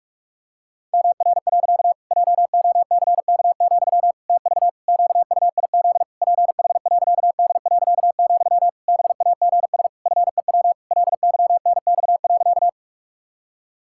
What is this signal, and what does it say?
Morse code, heard as a keyed tone, about 46 wpm, 700 Hz. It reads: MR. JOCK, TV QUIZ PH.D., BAGS FEW LYNX.
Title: Pangram Challenge v2 - 45wpm